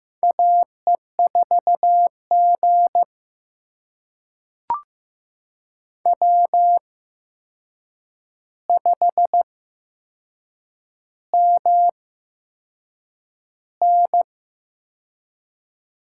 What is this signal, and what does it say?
Morse code, heard as a keyed tone, about 15 wpm, spaced as for 6 wpm, 700 Hz. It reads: AE4G W5MN
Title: USA Call Signs - 4 Characters - 8x Character Spacing - 15wpm